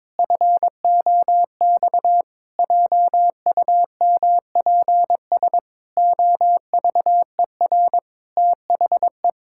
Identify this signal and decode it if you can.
Morse code, heard as a keyed tone, about 22 wpm, 700 Hz. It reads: FOX JUMPS OVER THE
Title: Pangram Challenge v2 - 22wpm